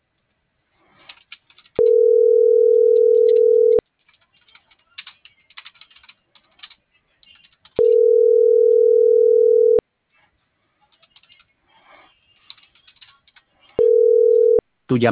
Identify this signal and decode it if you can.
Ringback tone (North America), 440 + 480 Hz.